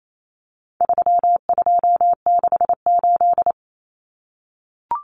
Morse code, heard as a keyed tone, about 28 wpm, 700 Hz. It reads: 3268
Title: Numbers - 4-Digit 28wpm